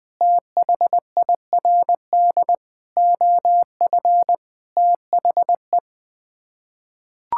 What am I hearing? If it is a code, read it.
Morse code, heard as a keyed tone, about 21 wpm, 700 Hz. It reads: THIRD OF THE
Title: Sets of 6 Words - 20wpm